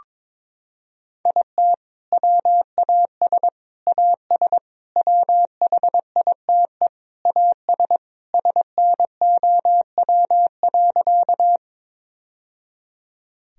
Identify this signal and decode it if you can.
Morse code, heard as a keyed tone, about 22 wpm, 700 Hz. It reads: IT WAS AS WHITE AS SNOW.
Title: Sentences from Top 1000 Words - Review Part 3 - 22wpm